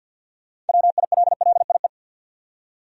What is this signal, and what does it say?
Morse code, heard as a keyed tone, about 52 wpm, 700 Hz. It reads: WILLIE
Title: USA Common Names - 50wpm